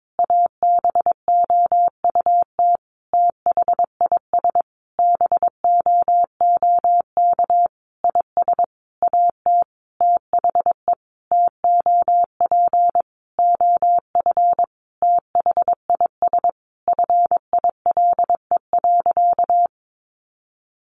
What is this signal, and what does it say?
Morse code, heard as a keyed tone, about 22 wpm, 700 Hz. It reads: ABOUT THIS BOOK IS AT THE TOP OF THIS FILE.
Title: Sentences from Top 1200 Words 22wpm